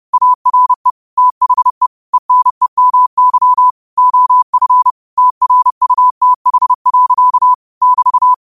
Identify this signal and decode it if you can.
Morse code, heard as a keyed tone, about 30 wpm, 1000 Hz. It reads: ARE THE ENEMY OF TRUTH. <BT>
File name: 30wpm